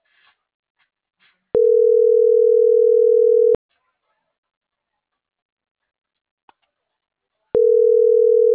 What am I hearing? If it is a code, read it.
Ringback tone (North America), 440 + 480 Hz.